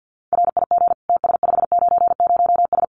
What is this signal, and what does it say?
Morse code, heard as a keyed tone, about 50 wpm, 700 Hz. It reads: RIG TS590S
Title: Common QSO Phrases Encore 50wpm